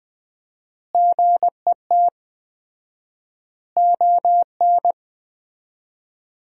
Morse code, heard as a keyed tone, about 20 wpm, 700 Hz. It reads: GET ON
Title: 4x Word-Spacing - Sentences from Top 200 Words - 20wpm